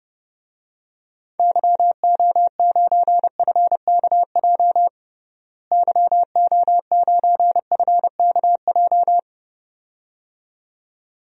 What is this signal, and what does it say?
Morse code, heard as a keyed tone, about 32 wpm, 700 Hz. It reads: YO9FKJ YO9FKJ
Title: DXCC-100 Call Signs - 6 Characters - Repeated 2x - 30wpm